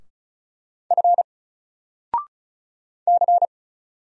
Morse code, heard as a keyed tone, about 35 wpm, 700 Hz. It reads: F C
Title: Single Letter-Number 35wpm